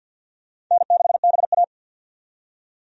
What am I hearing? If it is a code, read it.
Morse code, heard as a keyed tone, about 46 wpm, 700 Hz. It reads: N6BA
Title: USA Call Signs - 4 Characters - 50wpm